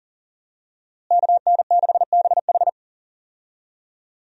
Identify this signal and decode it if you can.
Morse code, heard as a keyed tone, about 38 wpm, 700 Hz. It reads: KN6BH
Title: USA Call Signs - 5 Characters - Repeated 2x - 40wpm